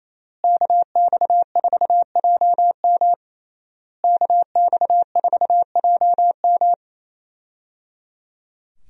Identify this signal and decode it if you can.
Morse code, heard as a keyed tone, about 28 wpm, 700 Hz. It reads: KX4JM KX4JM
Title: USA Call Signs - 5 Characters - Repeated 3x - 28wpm